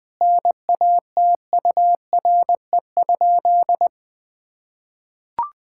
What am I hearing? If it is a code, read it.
Morse code, heard as a keyed tone, about 20 wpm, 700 Hz. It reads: NATURE?
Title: Sentences from Top 1000 Words - Review Part 5 - 20wpm